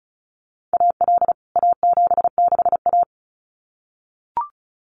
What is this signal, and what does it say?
Morse code, heard as a keyed tone, about 34 wpm, 700 Hz. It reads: AL A76A
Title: CWT Exchange 35wpm